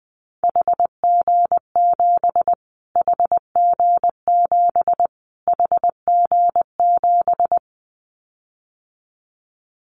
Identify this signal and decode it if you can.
Morse code, heard as a keyed tone, about 20 wpm, 700 Hz. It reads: HG7 HG7 HG7
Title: Callsign Prefixes for Top 25 DXCC - Repeated 3x - 20wpm